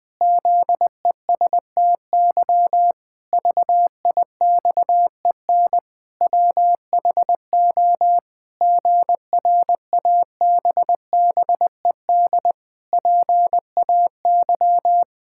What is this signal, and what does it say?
Morse code, heard as a keyed tone, about 20 wpm, 700 Hz. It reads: ZESTY VIXEN WHO GRABBED PAY
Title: Pangram Challenge v2 - 20wpm